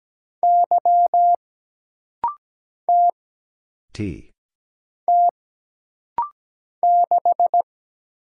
Morse code, heard as a keyed tone, about 17 wpm, 700 Hz. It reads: Y T T 6
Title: Single Letter-Number Alt - Rapid-Fire 17wpm